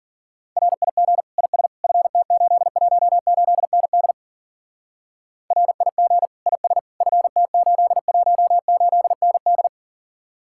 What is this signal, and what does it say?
Morse code, heard as a keyed tone, about 50 wpm, 700 Hz. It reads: RIG IS FT818ND RIG IS FT818ND